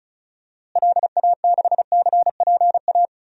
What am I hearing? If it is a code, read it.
Morse code, heard as a keyed tone, about 34 wpm, 700 Hz. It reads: LA6CPA
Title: DXCC-100 Call Signs - 6 Characters - Repeated 2x - 35wpm